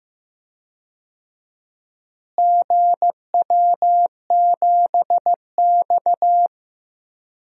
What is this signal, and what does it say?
Morse code, heard as a keyed tone, about 15 wpm, 700 Hz. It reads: GW7X
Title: DXCC-100 Call Signs - 4 Characters - Repeated 3x - 15wpm